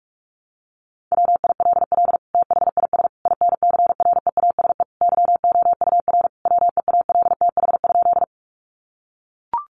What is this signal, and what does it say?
Morse code, heard as a keyed tone, about 44 wpm, 700 Hz. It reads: WILL THIS INCREASE YOUR WEALTH?
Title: Sentences from Top 1200 Words 45wpm